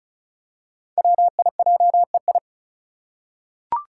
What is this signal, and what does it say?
Morse code, heard as a keyed tone, about 36 wpm, 700 Hz. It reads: WIJEI
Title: Five Challenging Letter-Number Combinations 35wpm